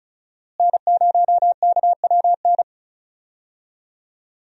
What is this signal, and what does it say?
Morse code, heard as a keyed tone, about 34 wpm, 700 Hz. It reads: N0KWN
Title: USA Call Signs - 5 Characters - Repeated 2x - 35wpm